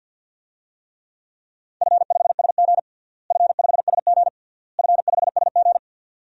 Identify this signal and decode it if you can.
Morse code, heard as a keyed tone, about 49 wpm, 700 Hz. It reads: F5SG F5SG F5SG
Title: DXCC-100 Call Signs - 4 Characters - Repeated 3x - 50wpm